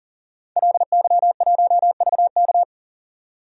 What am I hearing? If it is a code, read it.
Morse code, heard as a keyed tone, about 40 wpm, 700 Hz. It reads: LY1VK
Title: DXCC-100 Call Signs - 5 Characters - 40wpm